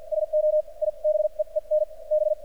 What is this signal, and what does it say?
Morse code, heard as a keyed tone, about 46 wpm, spaced as for 38 wpm, 610 Hz. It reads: SO I BEEN B
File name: archived / CRSnet